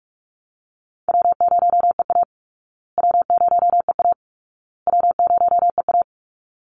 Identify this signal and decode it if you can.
Morse code, heard as a keyed tone, about 44 wpm, 700 Hz. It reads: W0EA W0EA W0EA
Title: Easy Call Signs Repeated 3x - 45wpm